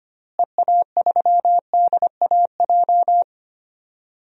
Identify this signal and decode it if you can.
Morse code, heard as a keyed tone, about 25 wpm, 700 Hz. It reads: EA3DAJ
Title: DXCC-100 Call Signs - 6 Characters - Repeated 3x - 25wpm